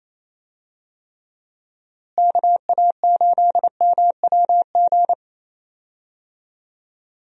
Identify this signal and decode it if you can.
Morse code, heard as a keyed tone, about 28 wpm, 700 Hz. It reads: KA8MWG
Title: USA Call Signs - 6 Characters - Repeated 2x - 28wpm